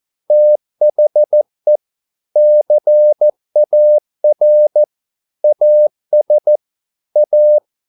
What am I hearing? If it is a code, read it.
Morse code, heard as a keyed tone, about 14 wpm, 600 Hz. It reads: THE CAR AS A